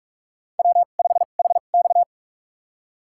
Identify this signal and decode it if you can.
Morse code, heard as a keyed tone, about 46 wpm, spaced as for 33 wpm, 700 Hz. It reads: W5HX
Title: USA Call Signs - 4 Characters - 2x Character Spacing - 45wpm